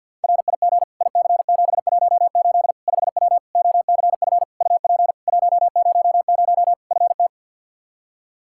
Morse code, heard as a keyed tone, about 53 wpm, 700 Hz. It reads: RIG IC718 5W OCF UP 100 FT
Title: K1USN SST QSO Practice 50wpm